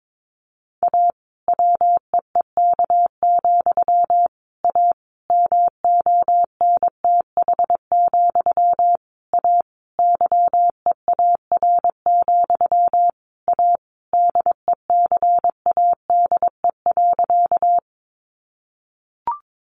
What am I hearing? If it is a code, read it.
Morse code, heard as a keyed tone, about 22 wpm, 700 Hz. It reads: A WEEK, A MONTH, A YEAR, A DECADE.
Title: A Sound Of Thunder 22wpm